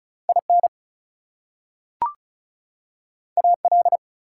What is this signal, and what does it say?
Morse code, heard as a keyed tone, about 36 wpm, 700 Hz. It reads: IN AL
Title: US States and Canadian Provinces 35wpm